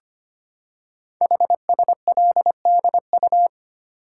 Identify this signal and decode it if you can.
Morse code, heard as a keyed tone, about 25 wpm, 700 Hz. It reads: HSLDU
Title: Code Groups - 5 Letter:Numbers - 25wpm